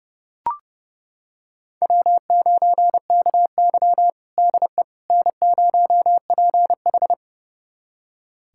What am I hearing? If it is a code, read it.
Morse code, heard as a keyed tone, about 32 wpm, 700 Hz. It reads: W9KY DE N0PH
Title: USA Callsign DE Callsign - 4 Characters - 30wpm